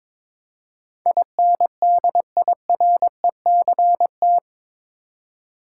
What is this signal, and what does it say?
Morse code, heard as a keyed tone, about 22 wpm, 700 Hz. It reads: INDIRECT
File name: practice / 22wpm / Eight Letter Words - 22wpm